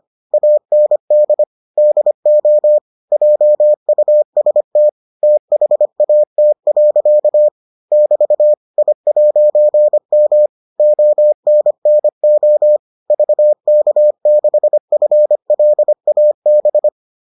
Morse code, heard as a keyed tone, about 25 wpm, 600 Hz. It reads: AND DO JUST THAT. <BT> I'M ONNO VK6FLAB